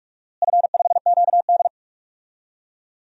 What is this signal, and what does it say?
Morse code, heard as a keyed tone, about 44 wpm, 700 Hz. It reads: F5QD
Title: DXCC-100 Call Signs - 4 Characters - Repeated 2x - 45wpm